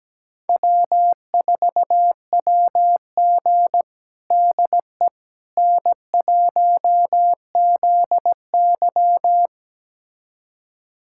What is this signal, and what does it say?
Morse code, heard as a keyed tone, about 17 wpm, 700 Hz. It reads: W4WG DE N1ZY
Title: USA Callsign DE Callsign - 4 Characters - 17wpm